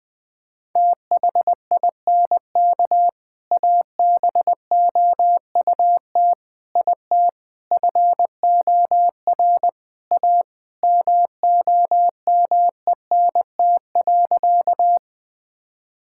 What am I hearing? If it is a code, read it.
Morse code, heard as a keyed tone, about 20 wpm, 700 Hz. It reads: THINK ABOUT IT FOR A MOMENT.
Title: Sentences with 3-Word Phrases - Spoken Beginning - 20wpm